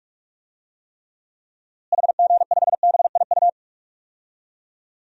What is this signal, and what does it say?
Morse code, heard as a keyed tone, about 46 wpm, 700 Hz. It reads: HG5BIU
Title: DXCC-100 Call Signs - 6 Characters - Repeated 3x - 45wpm